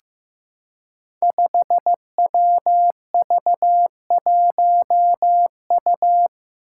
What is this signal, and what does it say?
Morse code, heard as a keyed tone, about 15 wpm, 700 Hz. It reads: 5WV1U
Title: Five Challenging Letter-Number Combinations 15wpm